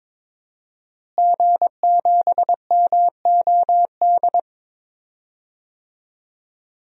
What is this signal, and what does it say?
Morse code, heard as a keyed tone, about 22 wpm, 700 Hz. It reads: G7MOD
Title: DXCC-100 Call Signs - 5 Characters - Repeated 3x - 22wpm